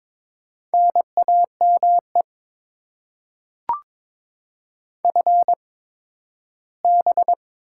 Morse code, heard as a keyed tone, about 22 wpm, spaced as for 13 wpm, 700 Hz. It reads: NAME FB
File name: Top 20 QSO Elements - 8x Character Spacing - 22wpm